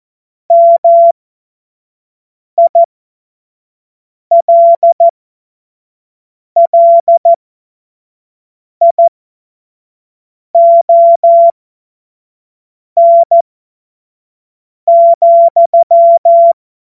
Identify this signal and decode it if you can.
Morse code, heard as a keyed tone, about 14 wpm, spaced as for 5 wpm, 680 Hz. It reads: MILLION,
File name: Morsekurs / Witze / Witz 10 Juni 2025 25 F70